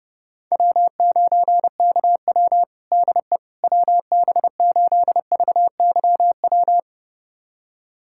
Morse code, heard as a keyed tone, about 30 wpm, 700 Hz. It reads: W9KW DE WB8VYW
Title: USA Callsign DE Callsign 30wpm